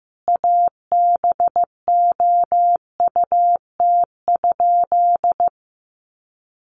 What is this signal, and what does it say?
Morse code, heard as a keyed tone, about 15 wpm, 700 Hz. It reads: ABOUT?
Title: Sentences from Top 1100 Words 15wpm